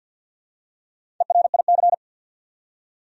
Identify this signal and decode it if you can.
Morse code, heard as a keyed tone, about 51 wpm, 700 Hz. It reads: ERIC
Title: USA Common Names - Max 5 Letters - 50wpm